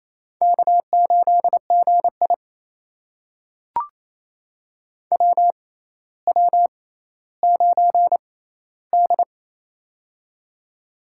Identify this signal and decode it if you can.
Morse code, heard as a keyed tone, about 28 wpm, spaced as for 14 wpm, 700 Hz. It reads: K8GI WW9D
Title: USA Call Signs - 4 Characters - 6x Character Spacing - 28wpm